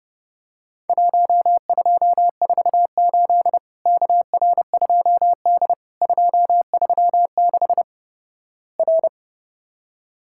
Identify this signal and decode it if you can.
Morse code, heard as a keyed tone, about 30 wpm, 700 Hz. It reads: 1248 KR2D 236 R
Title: Worked All Europe DX Contest - QTC Exchange - 30wpm